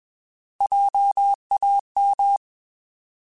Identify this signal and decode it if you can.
Morse code, heard as a keyed tone, about 21 wpm, 800 Hz. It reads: JAM
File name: Alphabetical / Alphabetical mp3s / Js @20